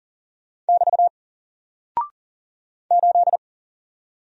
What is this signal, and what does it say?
Morse code, heard as a keyed tone, about 40 wpm, 700 Hz. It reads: <BT> 8
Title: Single Character (Letter:Number:Punctuation:Prosign) 40wpm